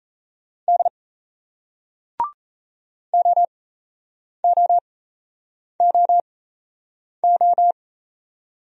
Morse code, heard as a keyed tone, about 38 wpm, spaced as for 8 wpm, 700 Hz. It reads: D OOOO